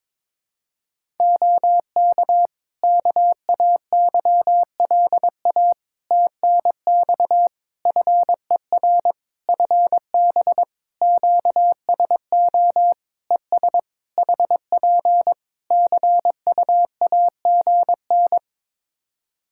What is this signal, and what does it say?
Morse code, heard as a keyed tone, about 23 wpm, 700 Hz. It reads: OK KAYLA TNX FER FB QSO ES HP CUAGN
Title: Practice QSO - Exchange 4 - Ending - 22wpm